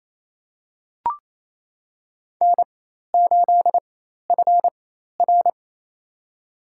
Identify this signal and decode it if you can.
Morse code, heard as a keyed tone, about 27 wpm, spaced as for 13 wpm, 700 Hz. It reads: N8FR
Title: USA Call Signs - 4 Characters - 4x Character Spacing - 28wpm